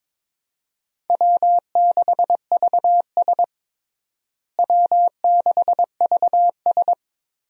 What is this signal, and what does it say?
Morse code, heard as a keyed tone, about 22 wpm, 700 Hz. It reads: W6VS W6VS
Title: USA Call Signs - 4 Characters - Repeated 3x - 22wpm